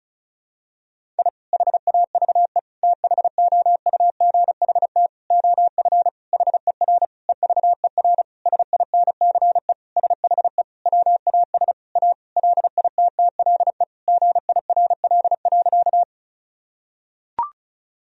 Morse code, heard as a keyed tone, about 34 wpm, 700 Hz. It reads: I HAVE THOUGHT OF HER EVER SINCE SHE WAS A LITTLE GIRL.
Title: Sentences from Top 1000 Words - Review Part 1 - 35wpm